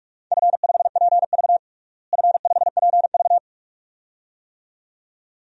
Morse code, heard as a keyed tone, about 42 wpm, 700 Hz. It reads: F5PV F5PV
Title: DXCC-100 Call Signs - 4 Characters - Repeated 2x - 45wpm